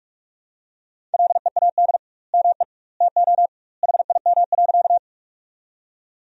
Morse code, heard as a keyed tone, about 42 wpm, 700 Hz. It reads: LEAD ME TO HIM.
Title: 2x Word-Spacing - Sentences from Top 500 Words - 45wpm